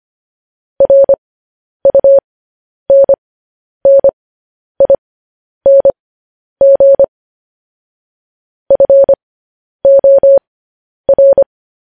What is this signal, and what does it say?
Morse code, heard as a keyed tone, about 25 wpm, spaced as for 10 wpm, 550 Hz. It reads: RUNNING FOR
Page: news-headlines-in-morse-code-at-10-wpm